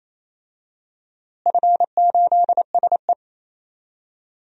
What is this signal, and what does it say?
Morse code, heard as a keyed tone, about 28 wpm, 700 Hz. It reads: F8SE
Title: DXCC-100 Call Signs - 4 Characters - Repeated 2x - 28wpm